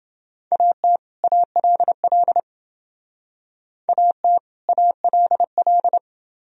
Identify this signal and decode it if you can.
Morse code, heard as a keyed tone, about 29 wpm, 700 Hz. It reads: AT ALL AT ALL